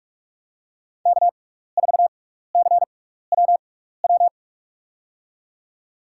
Morse code, heard as a keyed tone, about 46 wpm, spaced as for 15 wpm, 700 Hz. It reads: K4CWW
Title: USA Call Signs - 6x Character Spacing - 45wpm